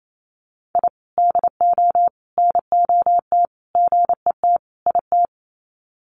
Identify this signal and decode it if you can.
Morse code, heard as a keyed tone, about 28 wpm, 700 Hz. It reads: I DO NOT GET IT
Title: Sets of 5 Words Encore 28wpm